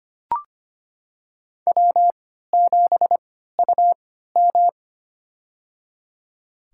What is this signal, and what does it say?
Morse code, heard as a keyed tone, about 25 wpm, spaced as for 14 wpm, 700 Hz. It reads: W7UM
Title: USA Call Signs - 3x Character Spacing - 25wpm